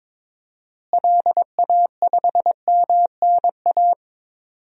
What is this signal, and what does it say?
Morse code, heard as a keyed tone, about 23 wpm, 700 Hz. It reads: LA5MNA
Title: DXCC-100 Call Signs - 6 Characters - Repeated 2x - 22wpm